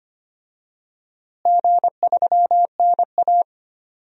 Morse code, heard as a keyed tone, about 25 wpm, 700 Hz. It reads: G3NA
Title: DXCC-100 Call Signs - 4 Characters - Repeated 2x - 25wpm